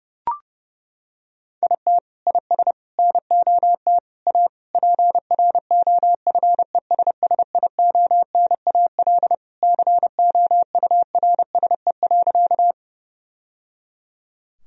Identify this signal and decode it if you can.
Morse code, heard as a keyed tone, about 32 wpm, 700 Hz. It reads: IT IS NOT A PROFESSIONAL COURSE.
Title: Sentences from Top 1000 Words - Review Part 4 - 30wpm